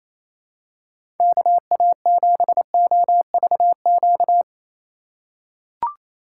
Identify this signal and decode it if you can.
Morse code, heard as a keyed tone, about 28 wpm, 700 Hz. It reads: KA7OVQ
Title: USA Call Signs - 6 Characters - 3x Character Spacing - 28wpm